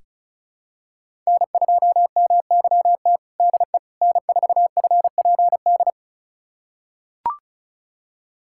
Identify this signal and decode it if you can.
Morse code, heard as a keyed tone, about 36 wpm, 700 Hz. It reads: N2MYT DE N4FPD
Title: USA Callsign DE Callsign - 5 Characters - 35wpm